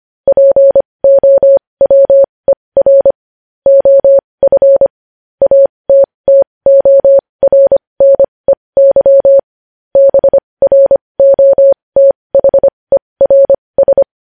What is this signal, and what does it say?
Morse code, heard as a keyed tone, about 25 wpm, spaced as for 20 wpm, 550 Hz. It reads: POWER OF ATTORNEY BROTHERS
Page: news-headlines-in-morse-code-at-20-wpm